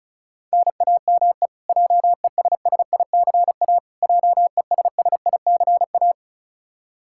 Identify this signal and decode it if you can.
Morse code, heard as a keyed tone, about 36 wpm, 700 Hz. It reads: NAME JESSICA JESSICA
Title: Practice QSO - Exchange 1 - Beginning - 35wpm